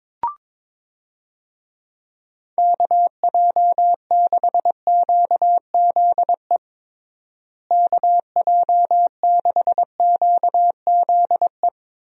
Morse code, heard as a keyed tone, about 22 wpm, 700 Hz. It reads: KJ6QZE KJ6QZE
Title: USA Call Signs - 6 Characters - Repeated 2x - 22wpm